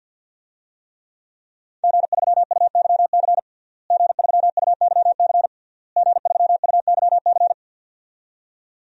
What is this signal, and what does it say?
Morse code, heard as a keyed tone, about 52 wpm, 700 Hz. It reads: G3UYC G3UYC G3UYC
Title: DXCC-100 Call Signs - 5 Characters - Repeated 3x - 50wpm